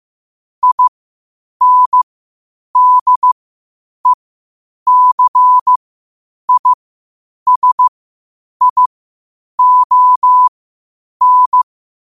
Morse code, heard as a keyed tone, about 15 wpm, spaced as for 8 wpm, 1000 Hz. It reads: INDECISION
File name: qotdfiles / 8wpm